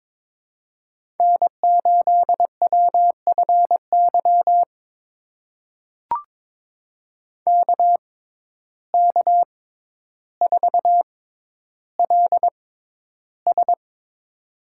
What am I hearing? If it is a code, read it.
Morse code, heard as a keyed tone, about 22 wpm, spaced as for 11 wpm, 700 Hz. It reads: N8WFY KK4LS
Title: USA Call Signs - 5 Characters - 6x Character Spacing - 22wpm